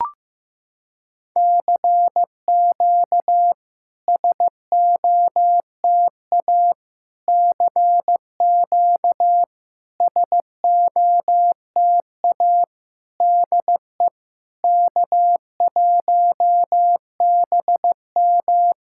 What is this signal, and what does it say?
Morse code, heard as a keyed tone, about 15 wpm, 700 Hz. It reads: CQ SOTA CQ SOTA DE K1BM
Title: SOTA QSO Practice 15wpm